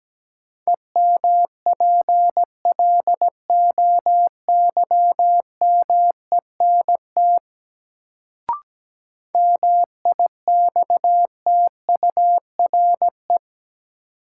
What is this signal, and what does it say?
Morse code, heard as a keyed tone, about 17 wpm, 700 Hz. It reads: EMPLOYMENT MIXTURE
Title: Sentences from Top 1500 Words - New Word Drill - 17wpm